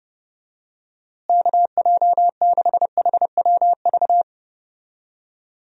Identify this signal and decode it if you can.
Morse code, heard as a keyed tone, about 32 wpm, 700 Hz. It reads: KJ6HWV
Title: USA Call Signs - 6 Characters - Repeated 2x - 30wpm